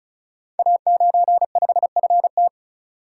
Morse code, heard as a keyed tone, about 36 wpm, 700 Hz. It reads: A95FT
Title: Code Groups - 5 Letter:Numbers - 35wpm